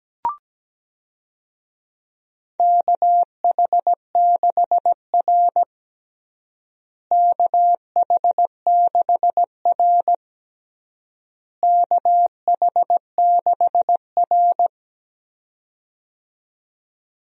Morse code, heard as a keyed tone, about 17 wpm, 700 Hz. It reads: KH6R KH6R KH6R
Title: USA Call Signs - 4 Characters - Repeated 3x - 17wpm